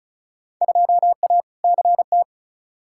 Morse code, heard as a keyed tone, about 36 wpm, 700 Hz. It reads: 2A CT
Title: Field Day Exchanges - 35wpm